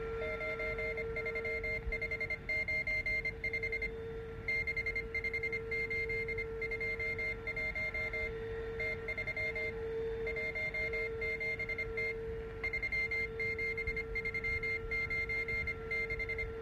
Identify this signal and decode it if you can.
Morse code, heard as a keyed tone, about 25 wpm, 2100 Hz. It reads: 93595 65821 T317T 37396